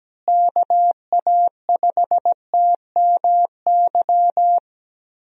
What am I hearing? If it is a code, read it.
Morse code, heard as a keyed tone, about 17 wpm, 700 Hz. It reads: KA5TMY
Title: USA Call Signs - 6 Characters - Repeated 3x - 17wpm